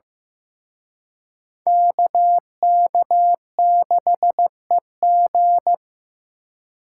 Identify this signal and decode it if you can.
Morse code, heard as a keyed tone, about 18 wpm, 700 Hz. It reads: KK6EG
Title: Speed Racing - USA 2x2 Call Signs 15wpm